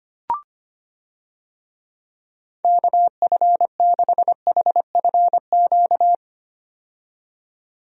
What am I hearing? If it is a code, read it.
Morse code, heard as a keyed tone, about 25 wpm, 700 Hz. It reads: KF6HFQ
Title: USA Call Signs - 6 Characters - 25wpm